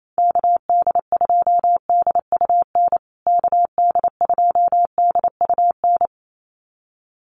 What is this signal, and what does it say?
Morse code, heard as a keyed tone, about 28 wpm, 700 Hz. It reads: KD2DUN KD2DUN
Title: USA Call Signs Repeated 2x 28wpm